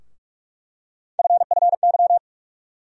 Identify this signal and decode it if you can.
Morse code, heard as a keyed tone, about 43 wpm, 700 Hz. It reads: FFY
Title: USA Call Sign Postfixes - 3 Letters 45wpm